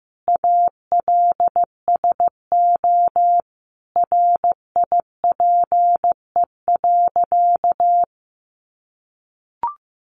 Morse code, heard as a keyed tone, about 15 wpm, 700 Hz. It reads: ALSO RIPE.
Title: Additional Easy Phrases 15wpm